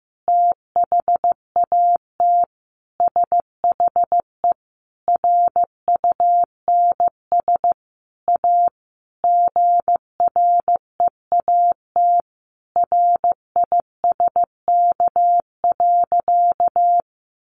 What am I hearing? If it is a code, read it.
Morse code, heard as a keyed tone, about 15 wpm, 700 Hz. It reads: THAT SHE RUNS A GREAT RISK.
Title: Sentences from Top 600 Words 15wpm